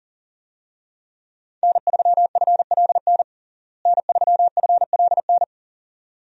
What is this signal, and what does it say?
Morse code, heard as a keyed tone, about 40 wpm, 700 Hz. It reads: N3FLN N3FLN
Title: USA Call Signs - 5 Characters - Repeated 2x - 40wpm